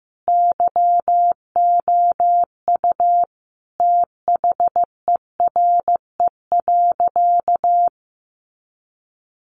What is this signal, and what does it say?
Morse code, heard as a keyed tone, about 14 wpm, 700 Hz. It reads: YOU THERE.
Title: A Sound Of Thunder 15wpm